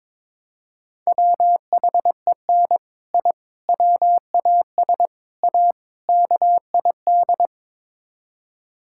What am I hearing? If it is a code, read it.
Morse code, heard as a keyed tone, about 22 wpm, 700 Hz. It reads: WHEN I WAS A KID
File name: practice / 22wpm / Build A Phrase - 5-Word Phrases - 22wpm